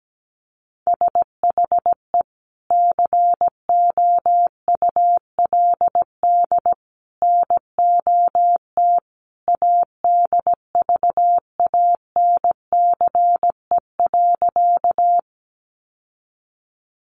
Morse code, heard as a keyed tone, about 17 wpm, 700 Hz. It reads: SHE COULD NOT ADVANCE.
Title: Sentences from Top 1400 Words 17wpm